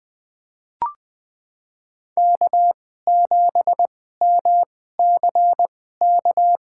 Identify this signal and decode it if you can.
Morse code, heard as a keyed tone, about 21 wpm, spaced as for 15 wpm, 700 Hz. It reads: K7MCK
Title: USA Call Signs - 5 Characters - 2x Character Spacing - 20wpm